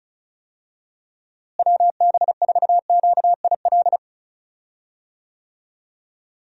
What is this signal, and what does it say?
Morse code, heard as a keyed tone, about 34 wpm, 700 Hz. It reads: WB4QIL
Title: USA Call Signs - 6 Characters - Repeated 3x - 35wpm